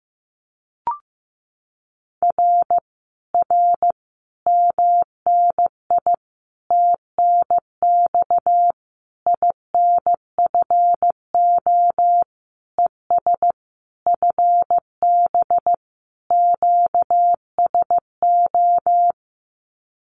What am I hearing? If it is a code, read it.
Morse code, heard as a keyed tone, about 15 wpm, 700 Hz. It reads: R R MNI TNX INFO ES FB QSO
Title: Common QSO Phrases Encore 15wpm